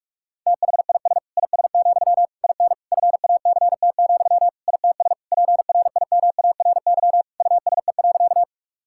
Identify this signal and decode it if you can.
Morse code, heard as a keyed tone, about 46 wpm, 700 Hz. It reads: THIS IS, IN FACT, ITS PRIMARY USE.
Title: Sentences from Top 1000 Words - Review Part 5 - 45wpm